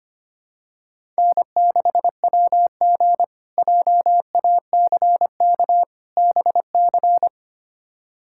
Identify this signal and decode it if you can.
Morse code, heard as a keyed tone, about 25 wpm, 700 Hz. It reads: N6WG JACK BC
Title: K1USN SST QSO Practice - 25wpm